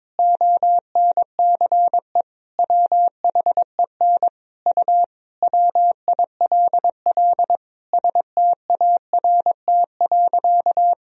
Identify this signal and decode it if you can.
Morse code, heard as a keyed tone, about 22 wpm, 700 Hz. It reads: ONCE WHEN U WILL START.
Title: Sentences from Top 300 Words and Abbreviations 22wpm